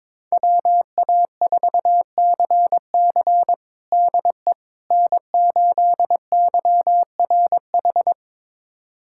Morse code, heard as a keyed tone, about 22 wpm, 700 Hz. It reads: WA4CC DE N8YRH
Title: USA Callsign DE Callsign - 5 Characters - 22wpm